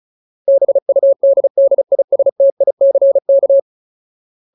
Morse code, heard as a keyed tone, about 34 wpm, 550 Hz. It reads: BUDDISTICK
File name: Ultimate Word List / 35WPM Ultimate